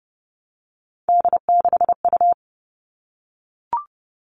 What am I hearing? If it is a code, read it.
Morse code, heard as a keyed tone, about 30 wpm, 700 Hz. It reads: D6U
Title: All Cut Numbers - 3-Digit 30wpm